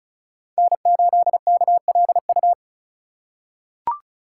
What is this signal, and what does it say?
Morse code, heard as a keyed tone, about 35 wpm, 690 Hz. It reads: N8KLU
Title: USA Call Signs - 5 Characters - 2x Character Spacing - 35wpm